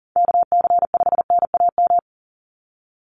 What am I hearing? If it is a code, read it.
Morse code, heard as a keyed tone, about 37 wpm, 690 Hz. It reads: KC5NAM
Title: USA Call Signs 40wpm